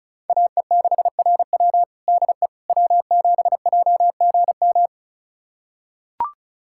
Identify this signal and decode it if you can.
Morse code, heard as a keyed tone, about 36 wpm, 700 Hz. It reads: AE6RW DE W7JGM
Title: USA Callsign DE Callsign - 5 Characters - 35wpm